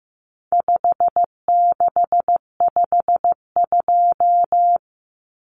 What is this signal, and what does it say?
Morse code, heard as a keyed tone, about 15 wpm, 700 Hz. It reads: 5652
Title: Common Cut Numbers - 4-Digit 15wpm